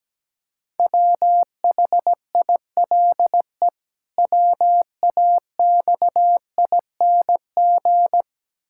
Morse code, heard as a keyed tone, about 17 wpm, 700 Hz. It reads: WHILE WAXING
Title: Pangram Challenge v2 - 17wpm